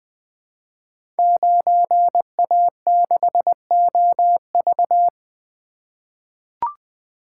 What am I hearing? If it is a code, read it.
Morse code, heard as a keyed tone, about 20 wpm, 700 Hz. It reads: 9A6OV
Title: Code Groups - 5 Letter:Numbers - 20wpm